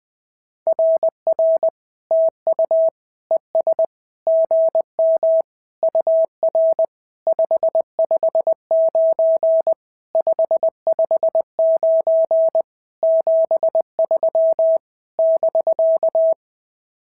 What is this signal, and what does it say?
Morse code, heard as a keyed tone, about 20 wpm, 650 Hz. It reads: RR TU ES GM UR 559 559 73 <BK>
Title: SOTA QSO Practice 20wpm